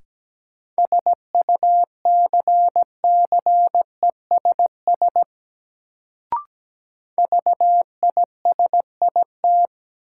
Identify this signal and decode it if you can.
Morse code, heard as a keyed tone, about 17 wpm, 700 Hz. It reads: SUCCESS VISIT
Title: Sentences from Top 900 Words - New Word Drill - 17wpm